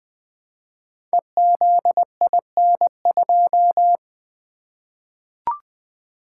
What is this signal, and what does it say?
Morse code, heard as a keyed tone, about 20 wpm, 700 Hz. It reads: EZIN2
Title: Code Groups - 5 Letter:Numbers - 20wpm